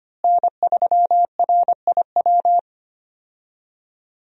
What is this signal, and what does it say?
Morse code, heard as a keyed tone, about 25 wpm, 700 Hz. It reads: N3RIW
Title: USA Call Signs - 5 Characters - Repeated 3x - 25wpm